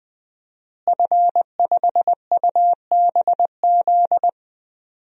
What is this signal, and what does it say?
Morse code, heard as a keyed tone, about 20 wpm, 700 Hz. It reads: F5UBZ
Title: DXCC-100 Call Signs - 5 Characters - Repeated 3x - 20wpm